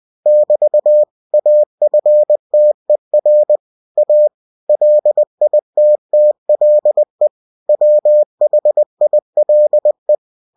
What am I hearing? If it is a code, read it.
Morse code, heard as a keyed tone, about 20 wpm, 600 Hz. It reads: <BT> AFTER A LITTLE WHILE
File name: cw-books / winnie-the-pooh / winnie-20wpm-page-0007